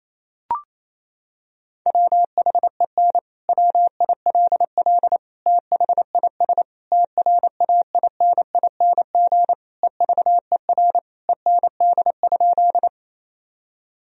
Morse code, heard as a keyed tone, about 28 wpm, 700 Hz. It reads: WHEN WILL THIS TRAINING EVER END?
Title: Sentences from Top 1000 Words - Review Part 4 - 28wpm